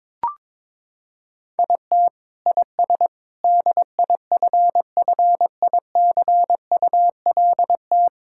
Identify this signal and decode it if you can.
Morse code, heard as a keyed tone, about 22 wpm, 700 Hz. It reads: IT IS DIFFICULT
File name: Sentences with 9-Word Phrases - 22wpm